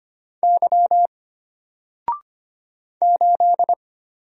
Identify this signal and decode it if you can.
Morse code, heard as a keyed tone, about 25 wpm, 700 Hz. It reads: Y 8
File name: Single-Letter-Number-25wpm